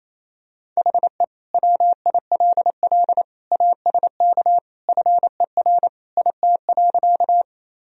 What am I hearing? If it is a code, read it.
Morse code, heard as a keyed tone, about 28 wpm, 700 Hz. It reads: HE WILL ASK FER IT.
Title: Sentences from Top 200 Words and Abbreviations 28wpm